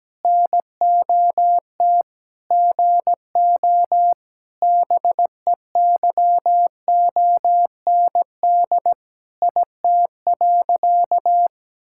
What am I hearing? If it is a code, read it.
Morse code, heard as a keyed tone, about 17 wpm, 700 Hz. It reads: NOT GO BEYOND IT.
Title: Sentences from Top 1000 Words - Review Part 6 - 17wpm